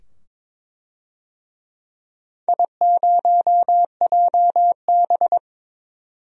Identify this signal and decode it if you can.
Morse code, heard as a keyed tone, about 22 wpm, 700 Hz. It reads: I0JB
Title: DXCC-100 Call Signs - 4 Characters - 22wpm